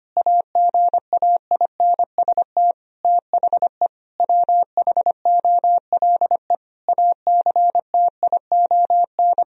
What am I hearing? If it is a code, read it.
Morse code, heard as a keyed tone, about 25 wpm, 700 Hz. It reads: AGAINST THE WHOLE ACTION
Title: Sentences from Top 1000 Words - Review Part 4 - 25wpm